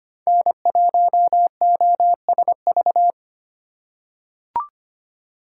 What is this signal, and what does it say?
Morse code, heard as a keyed tone, about 25 wpm, 700 Hz. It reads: N1OSV
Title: USA Call Signs - 4x Character Spacing - 25wpm